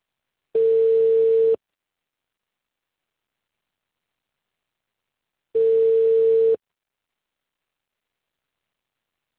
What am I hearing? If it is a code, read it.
Ringback tone (China, 450 Hz).